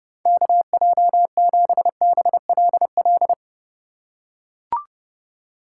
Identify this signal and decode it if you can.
Morse code, heard as a keyed tone, about 29 wpm, 700 Hz. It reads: KJ7BLL